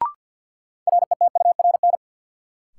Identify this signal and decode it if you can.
Morse code, heard as a keyed tone, about 53 wpm, 700 Hz. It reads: RETURN
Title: Sentences from Top 800 Words - New Word Drill - 50wpm